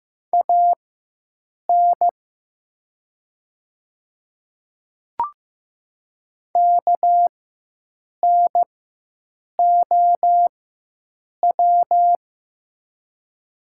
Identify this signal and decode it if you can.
Morse code, heard as a keyed tone, about 15 wpm, spaced as for 7 wpm, 700 Hz. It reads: AN KNOW
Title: 4x Character Spacing - Top 100 Words - 15wpm